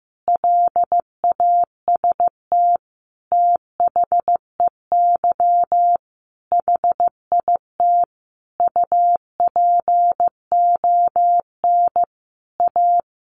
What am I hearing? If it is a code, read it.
Morse code, heard as a keyed tone, about 15 wpm, 700 Hz. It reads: LAST THEY HIT UPON A
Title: Sentences from Top 1200 Words 15wpm